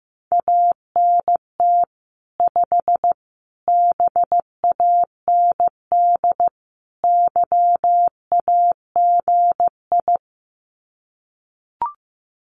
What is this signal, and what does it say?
Morse code, heard as a keyed tone, about 15 wpm, 700 Hz. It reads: ANT 5 BAND YAGI
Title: Common QSO Phrases 15wpm